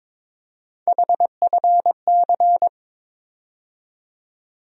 Morse code, heard as a keyed tone, about 21 wpm, 700 Hz. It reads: HFC
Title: USA Call Sign Postfixes - 3 Letters 22wpm